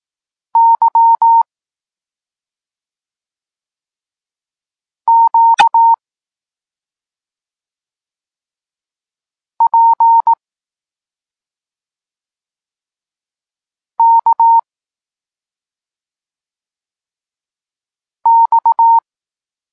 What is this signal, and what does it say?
Morse code, heard as a keyed tone, about 18 wpm, 920 Hz. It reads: Y Q P K X
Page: Tekst 32 – Doskonalenie odbioru poznanych liter